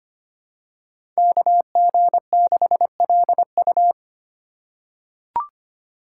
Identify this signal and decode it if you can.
Morse code, heard as a keyed tone, about 25 wpm, 700 Hz. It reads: KG6LU
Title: USA Call Signs - 4x Character Spacing - 25wpm